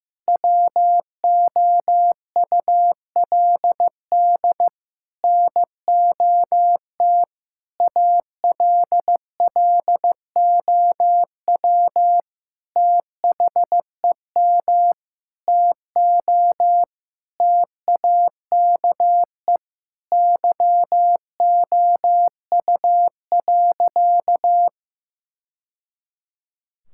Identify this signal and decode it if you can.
Morse code, heard as a keyed tone, about 15 wpm, 700 Hz. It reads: WOULD NOT ALLOW THEM TO TAKE YOU.
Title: Sentences from Top 1000 Words - Review Part 5 - 15wpm